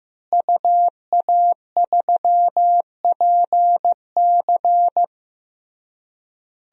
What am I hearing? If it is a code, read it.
Morse code, heard as a keyed tone, about 16 wpm, 700 Hz. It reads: UA3PC
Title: DXCC-100 Call Signs - 5 Characters - Repeated 3x - 15wpm